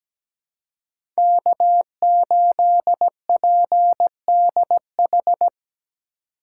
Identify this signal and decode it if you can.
Morse code, heard as a keyed tone, about 17 wpm, 700 Hz. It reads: K8PDH
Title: USA Call Signs - 5 Characters - 4x Character Spacing - 17wpm